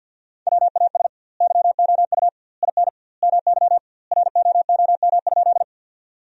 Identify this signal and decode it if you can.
Morse code, heard as a keyed tone, about 49 wpm, 700 Hz. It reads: WAS YOU IN MY ROOM?
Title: 2x Word-Spacing - Sentences from Top 300 Words - 50wpm